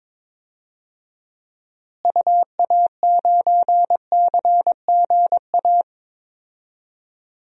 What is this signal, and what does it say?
Morse code, heard as a keyed tone, about 22 wpm, 700 Hz. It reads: UA9CGA
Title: DXCC-100 Call Signs - 6 Characters - Repeated 3x - 22wpm